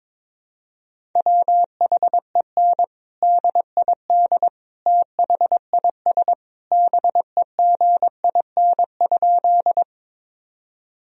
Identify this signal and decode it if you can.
Morse code, heard as a keyed tone, about 22 wpm, 700 Hz. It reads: WHEN DID THIS BEGIN?
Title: Sentences from Top 1000 Words - Review Part 1 - 22wpm